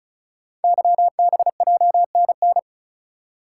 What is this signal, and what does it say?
Morse code, heard as a keyed tone, about 36 wpm, 690 Hz. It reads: YBJNN
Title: Code Groups - 5 Letters - 35wpm